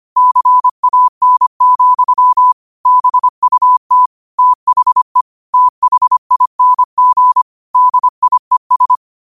Morse code, heard as a keyed tone, about 25 wpm, 1000 Hz. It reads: CAN, BUT THE THING DIES